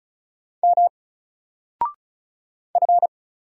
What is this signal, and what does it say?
Morse code, heard as a keyed tone, about 36 wpm, 700 Hz. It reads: M F
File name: Single-Letter-Number-35wpm